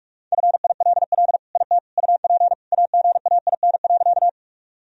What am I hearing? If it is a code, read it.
Morse code, heard as a keyed tone, about 46 wpm, 700 Hz. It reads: FILL IT UP AGAIN.
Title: Sentences from Top 1000 Words - Review Part 3 - 45wpm